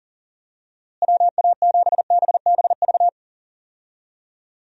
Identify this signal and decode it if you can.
Morse code, heard as a keyed tone, about 40 wpm, 700 Hz. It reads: WA7BBV
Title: USA Call Signs - 6 Characters - Repeated 3x - 40wpm